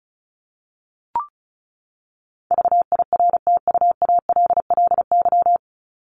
Morse code, heard as a keyed tone, about 34 wpm, 700 Hz. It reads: VIRTUALLY
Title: LY Suffix - Top 100 Words with Base Words 35wpm